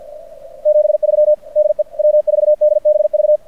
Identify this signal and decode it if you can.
Morse code, heard as a keyed tone, about 46 wpm, 590 Hz. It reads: 73 DE W4DBV
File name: archived / CRSnet